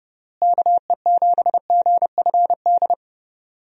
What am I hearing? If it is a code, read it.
Morse code, heard as a keyed tone, about 30 wpm, 700 Hz. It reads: KE7GFD
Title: USA Call Signs - 6 Characters - Repeated 3x - 30wpm